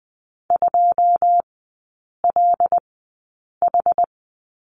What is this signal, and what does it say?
Morse code, heard as a keyed tone, about 20 wpm, spaced as for 8 wpm, 700 Hz. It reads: 2LH
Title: 081 Lesson 14e - Number 2 - Three Characters 20wpm